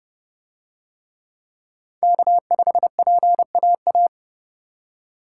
Morse code, heard as a keyed tone, about 29 wpm, 700 Hz. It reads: K5PAA